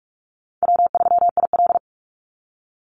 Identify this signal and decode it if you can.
Morse code, heard as a keyed tone, about 44 wpm, 700 Hz. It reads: W3IL
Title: Easy Call Signs Repeated 3x - 45wpm